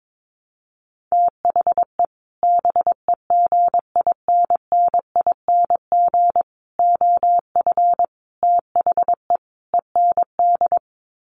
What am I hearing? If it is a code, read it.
Morse code, heard as a keyed tone, about 22 wpm, 700 Hz. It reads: THE BEGINNING OF THE END
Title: Sets of 5 Words Encore2 22wpm